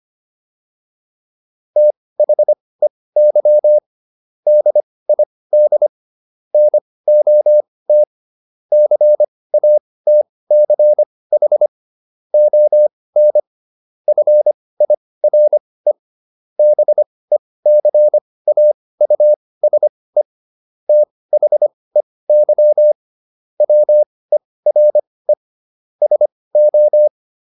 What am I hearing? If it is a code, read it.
Morse code, heard as a keyed tone, about 25 wpm, spaced as for 18 wpm, 600 Hz. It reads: THEY DID NOT CATCH ON FIRE BECAUSE THEY WERE SO